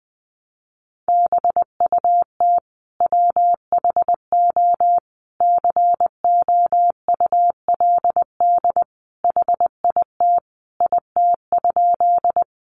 Morse code, heard as a keyed tone, about 20 wpm, 700 Hz. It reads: BUT WHO COULD HIT IT?
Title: Sentences from Top 1200 Words 20wpm